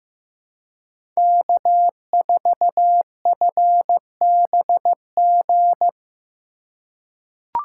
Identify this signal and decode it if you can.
Morse code, heard as a keyed tone, about 15 wpm, 700 Hz. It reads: K4FBG
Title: USA Call Signs - 5 Characters - 4x Character Spacing - 15wpm